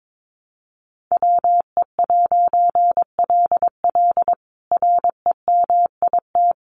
Morse code, heard as a keyed tone, about 22 wpm, 700 Hz. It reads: WE'LL REMIT
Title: A Sound Of Thunder 22wpm